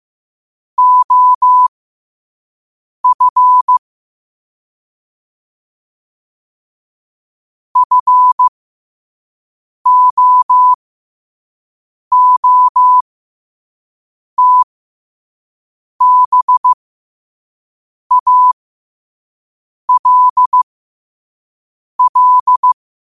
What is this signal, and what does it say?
Morse code, heard as a keyed tone, about 15 wpm, spaced as for 5 wpm, 1000 Hz. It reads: OF FOOTBALL
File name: qotdfiles / 5wpm